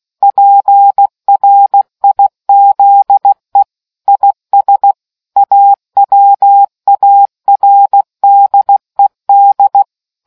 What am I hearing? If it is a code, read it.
Morse code, heard as a keyed tone, about 16 wpm, 800 Hz. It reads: PRIZE IS AWARDED